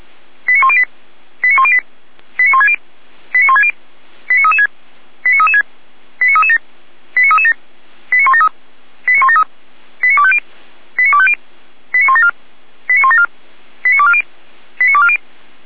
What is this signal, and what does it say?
Five-tone selective call (ZVEI), heard as quick runs of five tones, tones of about 70 ms. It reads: 89108 89108 89160 89160 89336 89336 89337 89337 89173 89173 89250 89250 89174 89174 89240 89240
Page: SDR/ZVEI